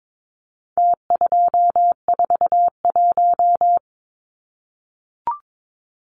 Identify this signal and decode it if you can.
Morse code, heard as a keyed tone, about 22 wpm, 700 Hz. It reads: T241
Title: Numbers - Contest Style - 4-Digit 22wpm